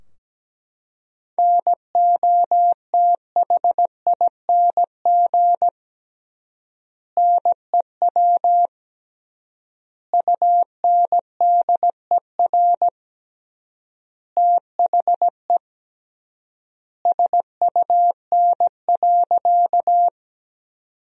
Morse code, heard as a keyed tone, about 17 wpm, 700 Hz. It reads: NOTHING NEW UNDER THE SUN.
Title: 3x Word-Spacing - Sentences from Top 500 Words - 17wpm